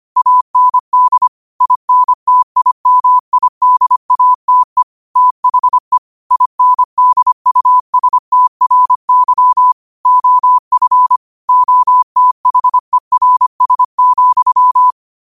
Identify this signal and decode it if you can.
Morse code, heard as a keyed tone, about 25 wpm, 1000 Hz. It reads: AND INTIMIDATE THE INDUSTRY OF OTHERS,
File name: qotdfiles / 25wpm